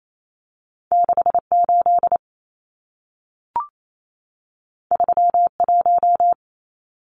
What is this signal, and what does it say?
Morse code, heard as a keyed tone, about 28 wpm, 700 Hz. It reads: 68 31
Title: Numbers - 2-Digit 28wpm